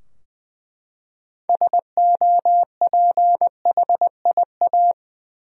Morse code, heard as a keyed tone, about 20 wpm, 700 Hz. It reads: SOPHIA
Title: USA Common Names - 20wpm